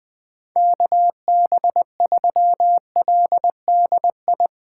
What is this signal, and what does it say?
Morse code, heard as a keyed tone, about 21 wpm, 700 Hz. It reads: KB3LDI
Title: USA Call Signs - 6 Characters - 2x Character Spacing - 20wpm